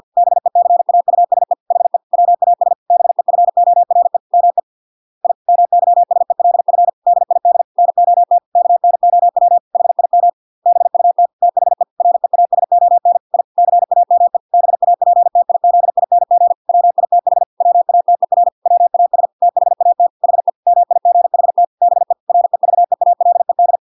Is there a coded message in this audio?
Yes: Morse code from 0.2 to 23.8 s, heard as a keyed tone, about 50 wpm, 700 Hz. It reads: BECAUSE HE WAS BEFORE ME I MYSELF DID NOT KNOW HIM BUT THE REASON I CAME BAPTIZING WITH WATER WAS THAT HE MIGHT BE REVEALED